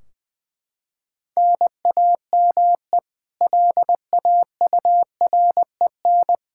Morse code, heard as a keyed tone, about 20 wpm, 700 Hz. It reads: NAME LAUREN
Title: Practice QSO - Exchange 1 - Beginning - 20wpm